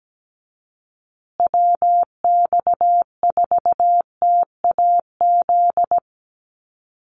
Morse code, heard as a keyed tone, about 17 wpm, 700 Hz. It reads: WX4TAZ
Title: USA Call Signs Repeated 2x 17wpm